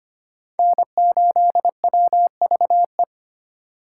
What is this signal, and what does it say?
Morse code, heard as a keyed tone, about 32 wpm, 700 Hz. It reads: N8WVE